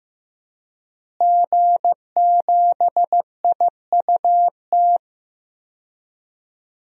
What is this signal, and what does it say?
Morse code, heard as a keyed tone, about 15 wpm, 700 Hz. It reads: G7IUT
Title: DXCC-100 Call Signs - 5 Characters - Repeated 2x - 15wpm